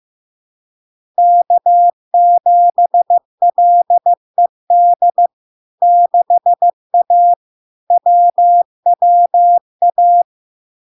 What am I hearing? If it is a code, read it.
Morse code, heard as a keyed tone, about 15 wpm, 700 Hz. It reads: K7LED 6A WWA
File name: LARK-FD-2022-Log-in-CW-15wpm-700-Hz